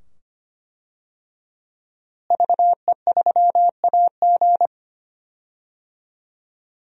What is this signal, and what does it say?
Morse code, heard as a keyed tone, about 25 wpm, 700 Hz. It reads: VE3AG
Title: DXCC-100 Call Signs - 5 Characters - Repeated 2x - 25wpm